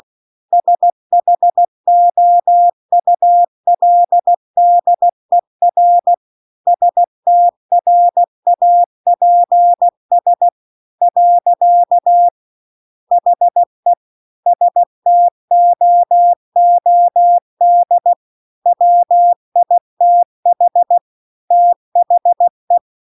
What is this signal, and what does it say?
Morse code, heard as a keyed tone, about 16 wpm, 700 Hz. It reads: SHOULDER STRAPS . HE STOOD WITH THE